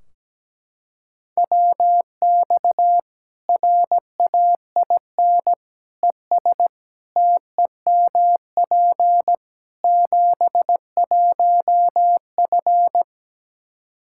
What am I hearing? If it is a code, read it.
Morse code, heard as a keyed tone, about 17 wpm, 700 Hz. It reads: WX RAIN ES TEMP 71F
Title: K1USN SST QSO Practice 17wpm